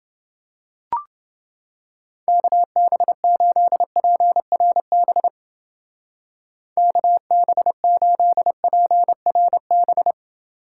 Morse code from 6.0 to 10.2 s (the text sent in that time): KB8PRB